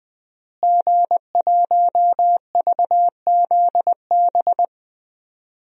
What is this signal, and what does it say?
Morse code, heard as a keyed tone, about 20 wpm, 700 Hz. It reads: G1VZB
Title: DXCC-100 Call Signs - 5 Characters - Repeated 3x - 20wpm